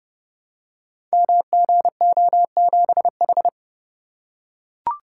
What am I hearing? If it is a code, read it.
Morse code, heard as a keyed tone, about 30 wpm, 700 Hz. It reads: MGO7H
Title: Code Groups - 5 Letter:Numbers - 30wpm